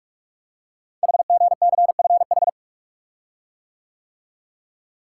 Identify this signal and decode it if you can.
Morse code, heard as a keyed tone, about 44 wpm, 700 Hz. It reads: HGCFH
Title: Code Groups - 5 Letters - 45wpm